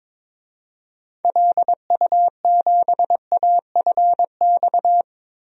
Morse code, heard as a keyed tone, about 22 wpm, 700 Hz. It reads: LU7AFX
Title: DXCC-100 Call Signs - 6 Characters - 22wpm